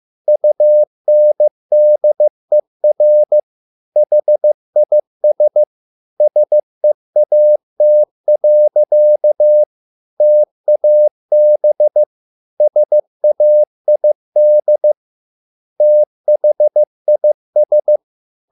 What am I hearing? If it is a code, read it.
Morse code, heard as a keyed tone, about 15 wpm, 600 Hz. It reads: UNDER HIS SEAT. TAB SAID THIS